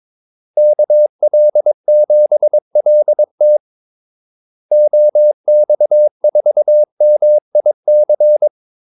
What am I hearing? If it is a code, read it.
Morse code, heard as a keyed tone, about 22 wpm, 600 Hz. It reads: KL7LT OX4MIC
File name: Callsigns-22wpm